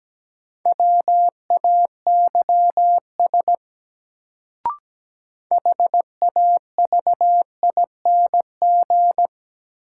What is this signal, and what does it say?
Morse code, heard as a keyed tone, about 17 wpm, 700 Hz. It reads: WAYS HAVING
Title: Sets of 5 Words Encore - New Word Drill - 17wpm